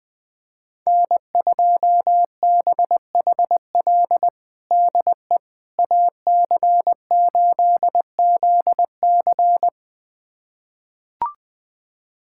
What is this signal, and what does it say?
Morse code, heard as a keyed tone, about 21 wpm, 700 Hz. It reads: N2BHL DE AC8ZC
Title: USA Callsign DE Callsign - 5 Characters - 20wpm